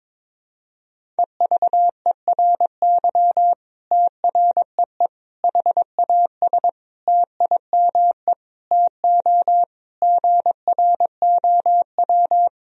Morse code, heard as a keyed tone, about 22 wpm, 700 Hz. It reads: EVERY TREE HAS TIME TO GROW
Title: Sentences from Top 1000 Words - Review Part 1 - 22wpm